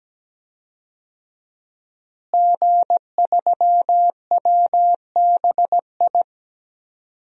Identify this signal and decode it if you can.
Morse code, heard as a keyed tone, about 17 wpm, 700 Hz. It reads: G3WBI
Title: DXCC-100 Call Signs - 5 Characters - Repeated 3x - 17wpm